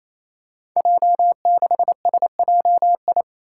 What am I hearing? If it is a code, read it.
Morse code, heard as a keyed tone, about 28 wpm, 700 Hz. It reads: J6SJI
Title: Five Challenging Letter-Number Combinations 28wpm